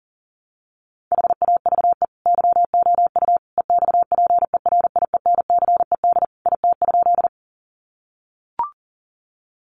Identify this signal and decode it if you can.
Morse code, heard as a keyed tone, about 40 wpm, 700 Hz. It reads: HAVE YOU EXPERIENCED IT?
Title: Sentences from Top 1100 Words 40wpm